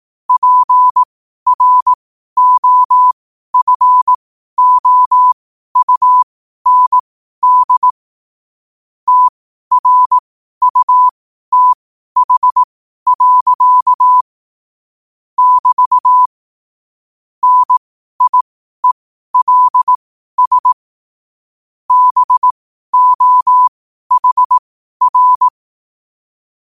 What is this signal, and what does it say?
Morse code, heard as a keyed tone, about 18 wpm, spaced as for 12 wpm, 1000 Hz. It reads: PROFOUND TRUTH. <BT> NIELS BOHR
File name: qotdfiles / 12wpm